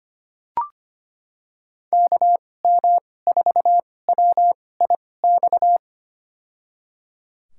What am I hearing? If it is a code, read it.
Morse code, heard as a keyed tone, about 25 wpm, spaced as for 18 wpm, 700 Hz. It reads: KM4WIX
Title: USA Call Signs - 6 Characters - 2x Character Spacing - 25wpm